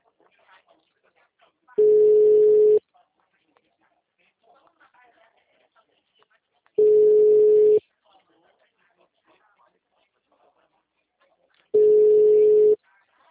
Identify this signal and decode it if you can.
Ringback tone (Europe, 425 Hz).
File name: out- -2007- - - .983